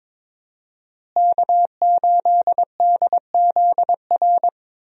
Morse code, heard as a keyed tone, about 22 wpm, 700 Hz. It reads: K8DZR
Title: USA Call Signs - 5 Characters - Repeated 2x - 22wpm